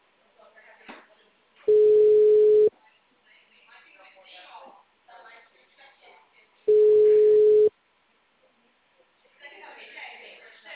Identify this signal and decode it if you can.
Ringback tone (Europe, 425 Hz), with speech.
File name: out- -2015- - - .66